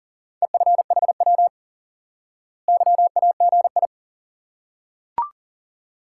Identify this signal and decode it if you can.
Morse code, heard as a keyed tone, about 40 wpm, 700 Hz. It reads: EFHW YAGI
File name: IQE Course 3 - Lesson 89 - YAGI - Two Elements - 40wpm